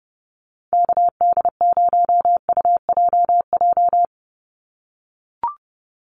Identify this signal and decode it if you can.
Morse code, heard as a keyed tone, about 30 wpm, 700 Hz. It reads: KD0UJJ
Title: USA Call Signs 30wpm